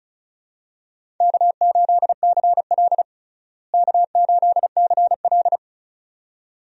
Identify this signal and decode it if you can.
Morse code, heard as a keyed tone, about 36 wpm, 700 Hz. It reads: K8CL K8CL
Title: USA Call Signs - 4 Characters - Repeated 2x - 35wpm